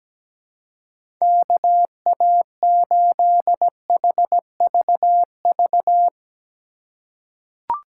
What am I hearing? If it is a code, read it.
Morse code, heard as a keyed tone, about 17 wpm, 700 Hz. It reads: KA8HVV
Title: USA Call Signs - 2x Character Spacing - 17wpm